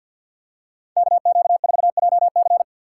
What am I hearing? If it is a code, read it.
Morse code, heard as a keyed tone, about 50 wpm, 700 Hz. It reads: KQ4JC
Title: USA Call Signs - 5 Characters - 4x Character Spacing - 50wpm